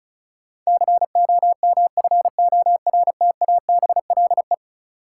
Morse code, heard as a keyed tone, about 34 wpm, 700 Hz. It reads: COMFORTABLE
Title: Sentences from Top 1000 Words - Review Part 7 - New Word Drill - 35wpm